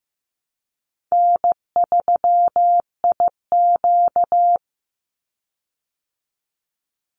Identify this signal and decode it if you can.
Morse code, heard as a keyed tone, about 15 wpm, 700 Hz. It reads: N3IQ
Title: Easy Call Signs Repeated 3x - 15wpm